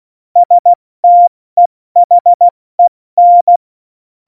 Morse code, heard as a keyed tone, about 16 wpm, 700 Hz. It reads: STEHEN
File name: Morsekurs / Witze / Witz 04 August 2020, 70 F80